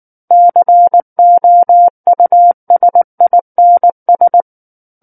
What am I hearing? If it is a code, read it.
Morse code, heard as a keyed tone, about 19 wpm, 700 Hz. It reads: COUSINS
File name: Book One / 10-Chapter 10 - 19 WPM